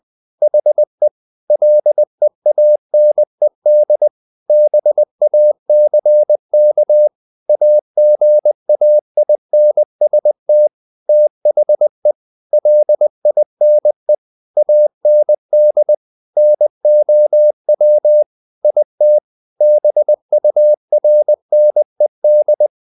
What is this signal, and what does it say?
Morse code, heard as a keyed tone, about 20 wpm, 600 Hz. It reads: HE LEANED BACK AGAINST THE LINE AND NOW IT BURNED
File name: the old man and the sea part 2 20wpm